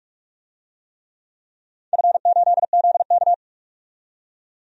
Morse code, heard as a keyed tone, about 43 wpm, 700 Hz. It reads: F8ZK
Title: DXCC-100 Call Signs - 4 Characters - 45wpm